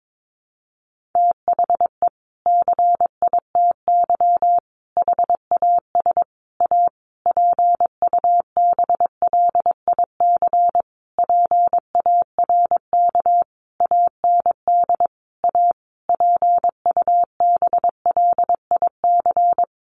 Morse code, heard as a keyed tone, about 22 wpm, 700 Hz. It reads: THE CITY HAS A PUBLIC PARK AND A PUBLIC
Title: Sentences from Top 1300 Words 22wpm